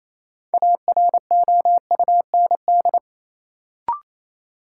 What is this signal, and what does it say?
Morse code, heard as a keyed tone, about 28 wpm, 700 Hz. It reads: AROUND
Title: Sets of 2 Words - New Word Drill - 28wpm